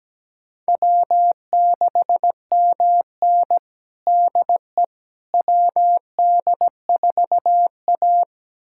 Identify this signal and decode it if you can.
Morse code, heard as a keyed tone, about 17 wpm, 700 Hz. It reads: W6MN DE WD4A
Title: USA Callsign DE Callsign - 4 Characters - 17wpm